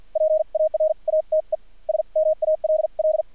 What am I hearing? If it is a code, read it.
Morse code, heard as a keyed tone, about 50 wpm, spaced as for 40 wpm, 630 Hz. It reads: PARATE SMALL
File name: archived / CRSnet